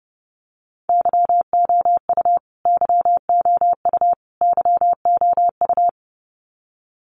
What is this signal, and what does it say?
Morse code, heard as a keyed tone, about 30 wpm, 700 Hz. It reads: YOU YOU YOU
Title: Top 500 Words Repeated 30wpm